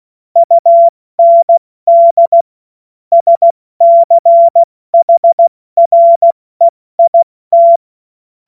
Morse code, heard as a keyed tone, about 16 wpm, spaced as for 14 wpm, 680 Hz. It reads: UND SCHREIT
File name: Witz-08-Juli-2025-70-F80